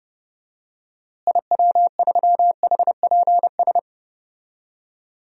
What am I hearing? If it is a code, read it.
Morse code, heard as a keyed tone, about 31 wpm, 700 Hz. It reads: IW3HPS
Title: DXCC-100 Call Signs - 6 Characters - Repeated 2x - 30wpm